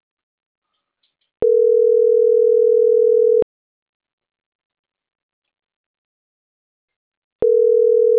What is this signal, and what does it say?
Ringback tone (North America), 440 + 480 Hz.